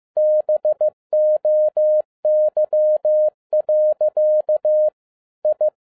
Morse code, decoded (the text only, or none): BOY. I